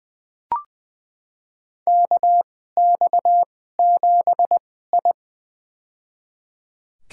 Morse code, heard as a keyed tone, about 20 wpm, spaced as for 15 wpm, 700 Hz. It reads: KX7I
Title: USA Call Signs - 4 Characters - 2x Character Spacing - 20wpm